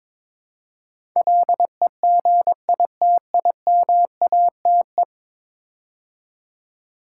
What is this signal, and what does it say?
Morse code, heard as a keyed tone, about 22 wpm, 700 Hz. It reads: LEGITIMATE
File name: practice / 22wpm / 10 Letter Words - First 3 Letters Given - 22wpm